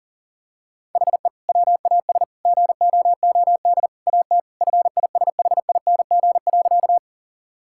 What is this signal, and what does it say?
Morse code, heard as a keyed tone, about 40 wpm, 700 Hz. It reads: HE WAS GOOD AT FISHING.
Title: Sentences from Top 1000 Words - Review Part 5 - 40wpm